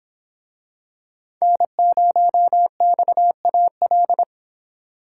Morse code, heard as a keyed tone, about 24 wpm, 700 Hz. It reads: N0XAL